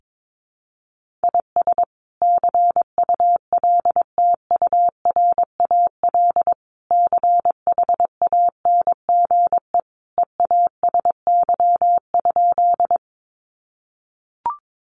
Morse code, heard as a keyed tone, about 22 wpm, 700 Hz. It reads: IS CULTURAL CHANGE EASY?
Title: Sentences from Top 1500 Words 22wpm